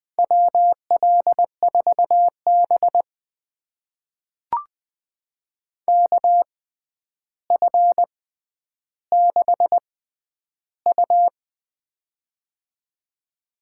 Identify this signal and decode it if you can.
Morse code, heard as a keyed tone, about 20 wpm, spaced as for 10 wpm, 700 Hz. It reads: WL4B KF6U
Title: USA Call Signs - 6x Character Spacing - 20wpm